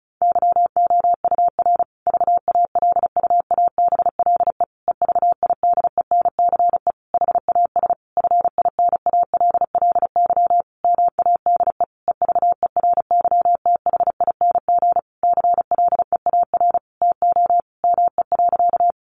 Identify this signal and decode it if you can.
Morse code, heard as a keyed tone, about 34 wpm, 700 Hz. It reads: YOUR VALUABLE EVIDENCE HAS FINALLY MADE EVERYTHING CLEAR TO ME.
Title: Sentences from Top 1300 Words 35wpm